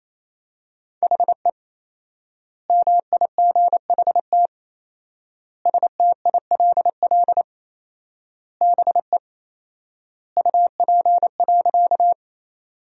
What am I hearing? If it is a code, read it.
Morse code, heard as a keyed tone, about 28 wpm, 700 Hz. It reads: HE MIGHT STILL BE UP.
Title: 4x Word-Spacing - Sentences from Top 300 Words - 28wpm